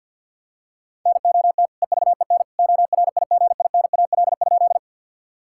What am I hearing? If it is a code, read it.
Morse code, heard as a keyed tone, about 52 wpm, 700 Hz. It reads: NOT EVEN ORIGINAL?
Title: Sentences from Top 1000 Words - Review Part 6 - 50wpm